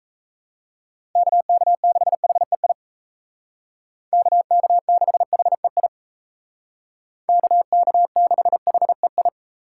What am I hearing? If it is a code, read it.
Morse code, heard as a keyed tone, about 35 wpm, 700 Hz. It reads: KK6HEI KK6HEI KK6HEI
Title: Speed Racing - USA Call Signs 28wpm